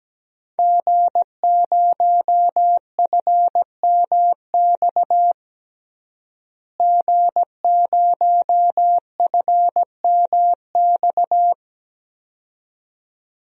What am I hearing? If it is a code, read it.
Morse code, heard as a keyed tone, about 17 wpm, 700 Hz. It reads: G0FMX G0FMX
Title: DXCC-100 Call Signs - 5 Characters - Repeated 2x - 17wpm